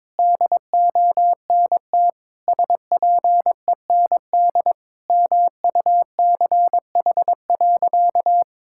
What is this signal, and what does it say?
Morse code, heard as a keyed tone, about 22 wpm, 700 Hz. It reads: DONT SPEND MUCH.
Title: Sentences from Top 1000 Words - Review Part 5 - 22wpm